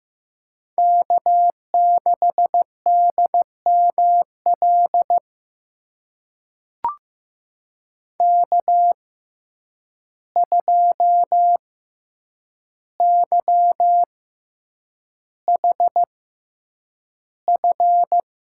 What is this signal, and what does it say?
Morse code, heard as a keyed tone, about 15 wpm, spaced as for 8 wpm, 700 Hz. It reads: K6DML K2YHF
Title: USA Call Signs - 5 Characters - 6x Character Spacing - 15wpm